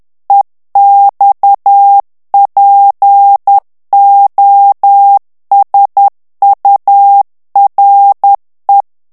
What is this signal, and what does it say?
Morse code, heard as a keyed tone, about 10 wpm, 800 Hz. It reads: EXPOSURE
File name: ANGLAIS texte clair 3 vitesse 10 mots minute